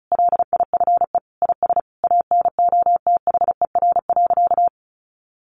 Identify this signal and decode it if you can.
Morse code, heard as a keyed tone, about 34 wpm, 700 Hz. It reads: LIFE IS ANOTHER.
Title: Sentences from Top 1000 Words 35wpm